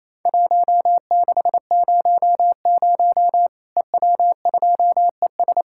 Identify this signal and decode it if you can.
Morse code, heard as a keyed tone, about 28 wpm, 700 Hz. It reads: 1600 EW2ES
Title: Worked All Europe DX Contest - QTC Exchange - 28wpm